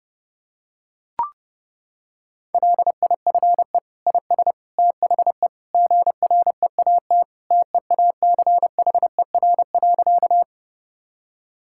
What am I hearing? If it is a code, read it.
Morse code, heard as a keyed tone, about 30 wpm, 700 Hz. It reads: LIFE IS THE GREAT TEACHER.
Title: Sentences from Top 1000 Words - Review Part 6 - 30wpm